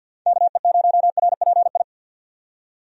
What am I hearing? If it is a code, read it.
Morse code, heard as a keyed tone, about 52 wpm, 700 Hz. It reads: KE0RPI
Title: USA Call Signs - 6 Characters - 50wpm